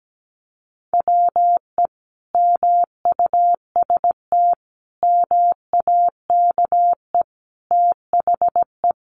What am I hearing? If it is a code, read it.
Morse code, heard as a keyed tone, about 17 wpm, 700 Hz. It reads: WE MUST MAKE THE
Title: Sentences from Top 400 Words 17wpm